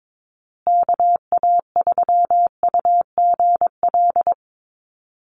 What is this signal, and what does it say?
Morse code, heard as a keyed tone, about 22 wpm, 700 Hz. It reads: KA3UGL
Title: USA Call Signs Repeated 2x 22wpm